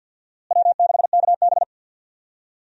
Morse code, heard as a keyed tone, about 51 wpm, 700 Hz. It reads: W6KB
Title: USA Call Signs - 4 Characters - Repeated 3x - 50wpm